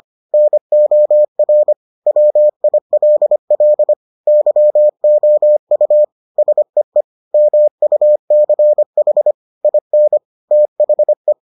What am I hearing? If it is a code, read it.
Morse code, heard as a keyed tone, about 25 wpm, 600 Hz. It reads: NOR WILL YOU SEE MUCH IN THE